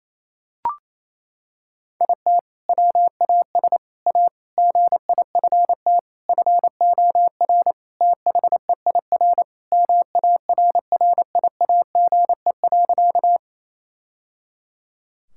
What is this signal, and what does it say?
Morse code, heard as a keyed tone, about 28 wpm, 700 Hz. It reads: IT WAS A GIFT FOR THEIR MARRIAGE.
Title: Sentences from Top 1000 Words - Review Part 6 - 28wpm